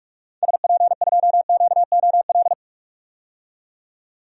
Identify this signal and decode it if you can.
Morse code, heard as a keyed tone, about 46 wpm, 700 Hz. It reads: SP2QOL
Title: DXCC-100 Call Signs - 6 Characters - 45wpm